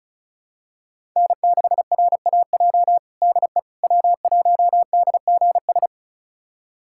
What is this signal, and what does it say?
Morse code, heard as a keyed tone, about 36 wpm, 700 Hz. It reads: N6RAJ DE W1DGS
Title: USA Callsign DE Callsign - 5 Characters - 35wpm